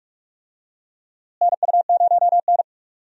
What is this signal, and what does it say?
Morse code, heard as a keyed tone, about 43 wpm, 700 Hz. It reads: NU0N
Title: USA Call Signs - 4 Characters - Repeated 2x - 45wpm